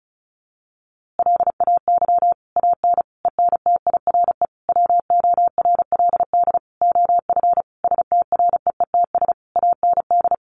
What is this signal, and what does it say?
Morse code, heard as a keyed tone, about 34 wpm, 700 Hz. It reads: LAY AN ENTIRE WORLD OF STREETS AND
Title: A Sound Of Thunder 35wpm